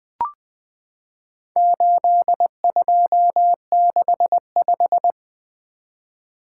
Morse code, heard as a keyed tone, about 20 wpm, 700 Hz. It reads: 8265
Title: Four Letter-Number Combinations - 20wpm